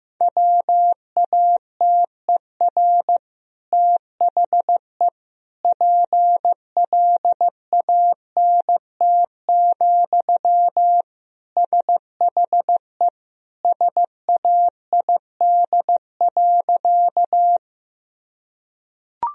Morse code, heard as a keyed tone, about 15 wpm, 700 Hz. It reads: WATER THE PLANT, SHE SAID.
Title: Sentences from Top 1000 Words - Review Part 1 - 15wpm